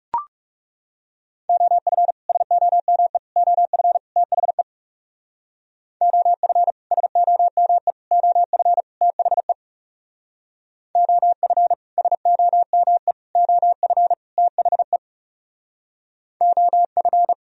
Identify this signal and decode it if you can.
Morse code, heard as a keyed tone, about 38 wpm, 700 Hz. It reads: OF SOME OF THE OF SOME OF THE OF SOME OF THE OF
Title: Speed-Racing - Sets of 4 Words 30wpm